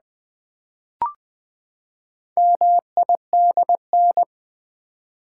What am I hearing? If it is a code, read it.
Morse code, heard as a keyed tone, about 20 wpm, 700 Hz. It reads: MIDN
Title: Build A Word - 4 Letters At A Time - 8-Letter Words - 20wpm